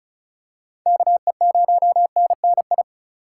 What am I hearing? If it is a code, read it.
Morse code, heard as a keyed tone, about 36 wpm, 700 Hz. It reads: KE0NNI
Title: USA Call Signs - 4x Character Spacing - 35wpm